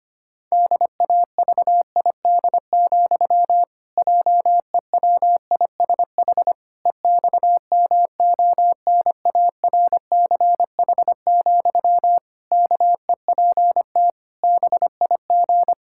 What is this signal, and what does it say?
Morse code, heard as a keyed tone, about 25 wpm, 700 Hz. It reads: DAVID, JEWISH EXMONARCH, KEPT BIG
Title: Pangram Challenge v2 - 25wpm